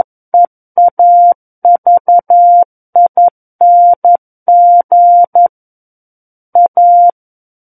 Morse code, heard as a keyed tone, about 11 wpm, 700 Hz. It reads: EAVING A
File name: Book One / 02-Chapter 2 - 11 WPM